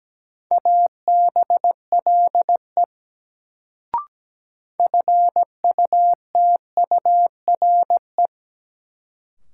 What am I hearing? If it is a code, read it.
Morse code, heard as a keyed tone, about 17 wpm, 700 Hz. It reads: ABLE FUTURE
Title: Sentences from Top 700 Words - New Word Drill - 17wpm